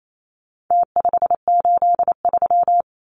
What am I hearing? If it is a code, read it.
Morse code, heard as a keyed tone, about 28 wpm, 700 Hz. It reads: T583
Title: Numbers - Contest Style - 4-Digit 28wpm